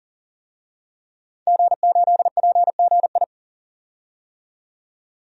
Morse code, heard as a keyed tone, about 40 wpm, 700 Hz. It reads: G8PGI
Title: DXCC-100 Call Signs - 5 Characters - 40wpm